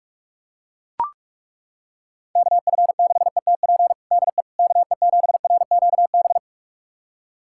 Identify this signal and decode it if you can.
Morse code, heard as a keyed tone, about 41 wpm, 700 Hz. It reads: KF6ETP DE KE7RQB
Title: USA Callsign DE Callsign - 6 Characters - 45wpm